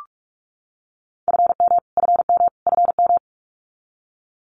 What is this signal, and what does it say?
Morse code, heard as a keyed tone, about 44 wpm, 700 Hz. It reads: FM FM FM
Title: Abbreviations Repeated 3x - 45wpm